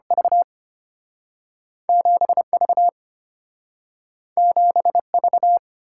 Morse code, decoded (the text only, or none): V 7V 7V